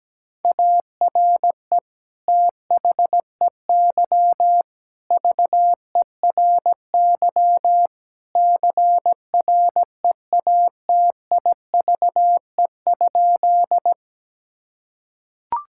Morse code, heard as a keyed tone, about 17 wpm, 700 Hz. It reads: ARE THEY VERY CREATIVE?
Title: Sentences from Top 1000 Words - Review Part 5 - 17wpm